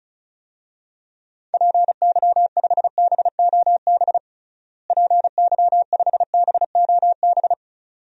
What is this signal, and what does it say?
Morse code, heard as a keyed tone, about 34 wpm, 700 Hz. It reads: PY5BOB PY5BOB
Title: DXCC-100 Call Signs - 6 Characters - Repeated 2x - 35wpm